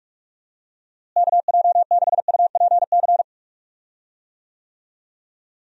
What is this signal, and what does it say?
Morse code, heard as a keyed tone, about 48 wpm, 700 Hz. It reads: KJ6UPC
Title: USA Call Signs - 6 Characters - Repeated 2x - 45wpm